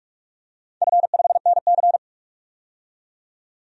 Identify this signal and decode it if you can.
Morse code, heard as a keyed tone, about 43 wpm, 700 Hz. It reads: F5NC
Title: DXCC-100 Call Signs - 4 Characters - 45wpm